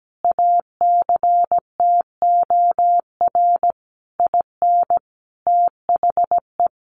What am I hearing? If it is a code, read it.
Morse code, heard as a keyed tone, about 17 wpm, 700 Hz. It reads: ACTOR IN THE
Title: Sentences from Top 1500 Words 17wpm